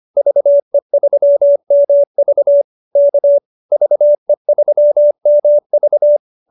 Morse code, heard as a keyed tone, about 25 wpm, 590 Hz. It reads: VE3MV K VE3MV